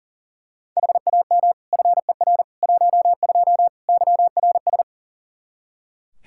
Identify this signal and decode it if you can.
Morse code, heard as a keyed tone, about 40 wpm, 700 Hz. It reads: HAM FER 12 YRS
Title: Practice QSO - Exchange 3 - More Info - 40wpm